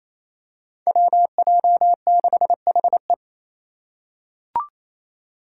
Morse code, heard as a keyed tone, about 27 wpm, 700 Hz. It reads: WJ6HE
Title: Five Challenging Letter-Number Combinations 28wpm